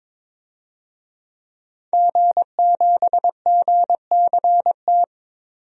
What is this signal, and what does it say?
Morse code, heard as a keyed tone, about 23 wpm, 700 Hz. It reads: G7GCT
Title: DXCC-100 Call Signs - 5 Characters - 22wpm